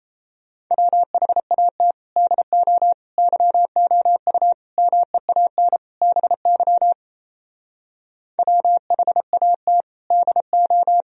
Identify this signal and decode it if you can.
Morse code, heard as a keyed tone, about 38 wpm, 700 Hz. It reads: WHAT DO YOU MEAN BY WHAT DO